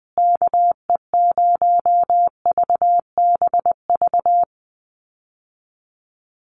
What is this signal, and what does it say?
Morse code, heard as a keyed tone, about 20 wpm, 700 Hz. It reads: KE0VBV
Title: USA Call Signs Repeated 3x 20wpm